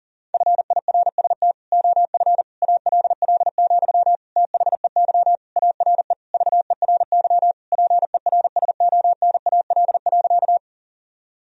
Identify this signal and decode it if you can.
Morse code, heard as a keyed tone, about 40 wpm, 700 Hz. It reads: FIRST OF ALL, THEY ARE VERY PERSONAL.
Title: Sentences from Top 1000 Words - Review Part 4 - 40wpm